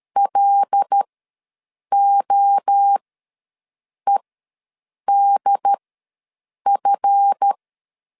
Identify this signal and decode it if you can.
Morse code, heard as a keyed tone, about 13 wpm, spaced as for 7 wpm, 800 Hz. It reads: LOEDF
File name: LEZIONI / 10 SIGNS / 10A SLASH ws 12wpm 3443